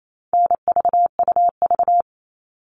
Morse code, heard as a keyed tone, about 28 wpm, 700 Hz. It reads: NVUV
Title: All Cut Numbers - 4-Digit 28wpm